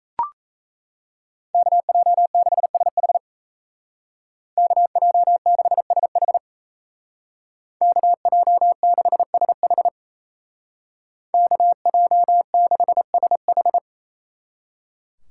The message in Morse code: KJ6SH KJ6SH KJ6SH KJ6SH

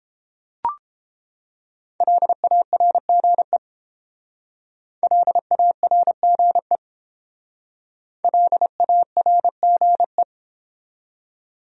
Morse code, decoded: LARGE LARGE LARGE

